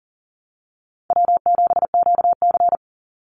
Morse code, heard as a keyed tone, about 40 wpm, 700 Hz. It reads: W7QC
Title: Call Signs Repeated 2x - 40wpm